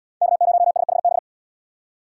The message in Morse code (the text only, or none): K0SHB